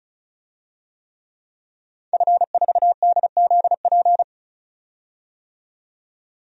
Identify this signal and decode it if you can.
Morse code, heard as a keyed tone, about 36 wpm, 700 Hz. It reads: F4DZP
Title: DXCC-100 Call Signs - 5 Characters - 35wpm